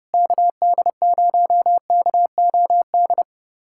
Morse code, heard as a keyed tone, about 30 wpm, 700 Hz. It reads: KD0KOD
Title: USA Callsign DE Callsign - 6 Characters - 30wpm